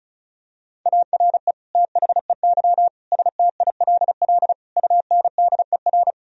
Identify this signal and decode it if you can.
Morse code, heard as a keyed tone, about 36 wpm, 700 Hz. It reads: ARE THEY STILL UNDER
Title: Sentences from Top 1000 Words - Review Part 1 - 35wpm